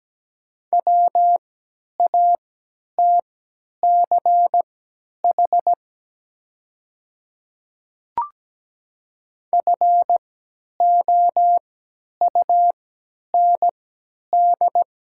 Morse code, heard as a keyed tone, about 17 wpm, spaced as for 10 wpm, 700 Hz. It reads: WATCH FOUND
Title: 3x Character Spacing - Top 500 Words - 17wpm